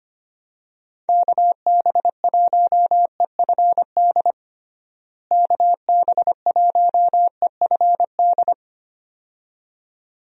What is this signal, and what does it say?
Morse code, heard as a keyed tone, about 25 wpm, 700 Hz. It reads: KB1EFD KB1EFD
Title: USA Call Signs - 6 Characters - Repeated 2x - 25wpm